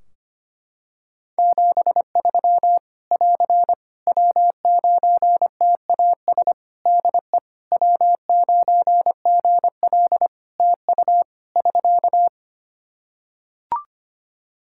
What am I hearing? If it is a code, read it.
Morse code, heard as a keyed tone, about 25 wpm, 700 Hz. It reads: 73 <AR> W9TAS DE W9GL TU <SK>
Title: Practice QSO - Exchange 4 - Ending - 25wpm